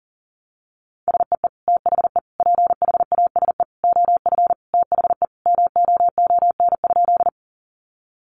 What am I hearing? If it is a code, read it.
Morse code, heard as a keyed tone, about 40 wpm, 700 Hz. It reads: SEE THE PHASE OF THE MOON?
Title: Sentences from Top 1200 Words 40wpm